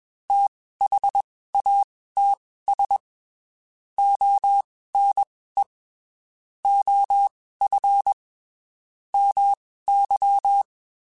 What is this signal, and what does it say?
Morse code, heard as a keyed tone, about 21 wpm, spaced as for 15 wpm, 800 Hz. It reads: THATS ONE OF MY
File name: (spaced) / QSOs Spaced-mp3s / WD0FBT 1 @20 sp